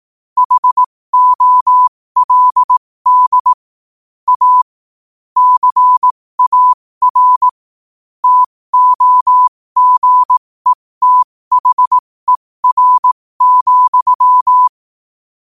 Morse code, heard as a keyed tone, about 18 wpm, spaced as for 15 wpm, 1000 Hz. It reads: HOLD A CAR TOGETHER,
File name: qotdfiles / 15wpm